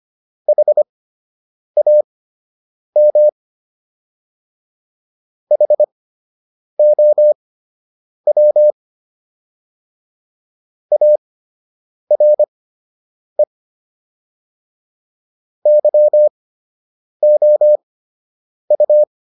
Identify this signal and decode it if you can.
Morse code, heard as a keyed tone, about 25 wpm, spaced as for 8 wpm, 600 Hz. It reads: HAM HOW ARE YOU